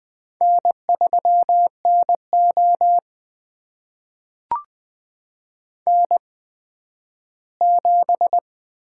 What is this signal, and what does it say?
Morse code, heard as a keyed tone, about 21 wpm, spaced as for 12 wpm, 700 Hz. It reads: N3NO N7
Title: USA Call Signs - 4 Characters - 8x Character Spacing - 20wpm